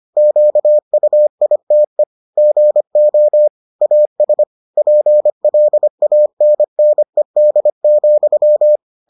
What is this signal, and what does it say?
Morse code, heard as a keyed tone, about 25 wpm, 600 Hz. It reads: QUITE GO AS PLANNED,